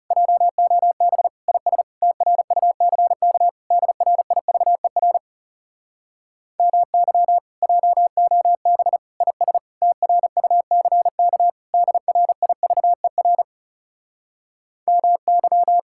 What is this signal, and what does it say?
Morse code, heard as a keyed tone, about 40 wpm, 700 Hz. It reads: JOB IS TRUCK DRIVER MY JOB IS TRUCK DRIVER MY